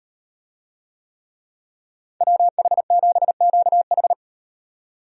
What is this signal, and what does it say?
Morse code, heard as a keyed tone, about 38 wpm, 700 Hz. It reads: WH7QH